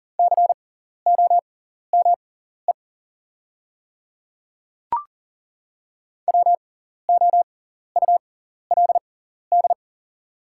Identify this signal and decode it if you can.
Morse code, heard as a keyed tone, about 40 wpm, spaced as for 14 wpm, 700 Hz. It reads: COME WOULD
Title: 6x Character Spacing - Top 100 Words - 40wpm